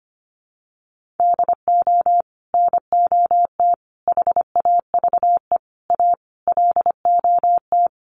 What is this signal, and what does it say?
Morse code, heard as a keyed tone, about 25 wpm, 700 Hz. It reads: DO NOT HAVE A LOT
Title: Sets of 5 Words Encore 25wpm